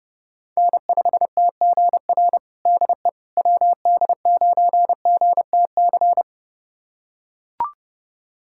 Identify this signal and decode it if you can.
Morse code, heard as a keyed tone, about 30 wpm, 700 Hz. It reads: N5TGR DE WD9GTC
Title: USA Callsign DE Callsign 30wpm